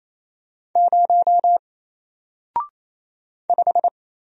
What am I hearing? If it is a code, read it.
Morse code, heard as a keyed tone, about 27 wpm, 700 Hz. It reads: 0 5
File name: Single-Letter-Number-28wpm